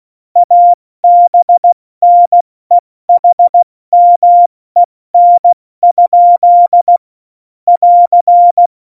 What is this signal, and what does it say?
Morse code, heard as a keyed tone, about 16 wpm, 700 Hz. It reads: ABNEHMEN? <AR>
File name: Morsekurs / Witze / Witz 13. Oktober 2020, 70 F80